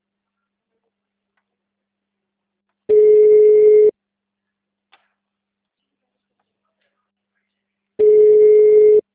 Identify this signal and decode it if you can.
Ringback tone (Europe, 425 Hz).
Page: Presidente Prudente - SP